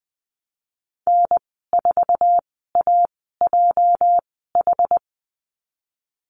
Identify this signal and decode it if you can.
Morse code, heard as a keyed tone, about 20 wpm, spaced as for 14 wpm, 700 Hz. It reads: N4AJH
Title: 238 Lesson 38g - Letter q - Call Signs 20wpm